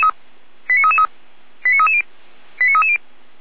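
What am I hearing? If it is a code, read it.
Five-tone selective call (ZVEI), heard as quick runs of five tones, tones of about 70 ms. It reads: 89333 89330 89330
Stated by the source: ZVEI five-tone selective call received by SDR, from the Feuerwehr Orlamünde fire brigade audio archive.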